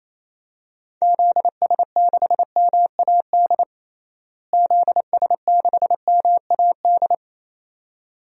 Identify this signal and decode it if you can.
Morse code, heard as a keyed tone, about 28 wpm, 700 Hz. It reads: ZS6MAD ZS6MAD
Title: DXCC-100 Call Signs - 6 Characters - Repeated 2x - 28wpm